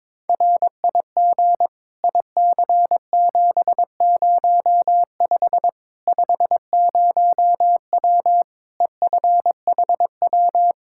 Morse code, heard as a keyed tone, about 22 wpm, 700 Hz. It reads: RIG IC705 50W EFHW
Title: Practice QSO - Exchange 2 - Info - 22wpm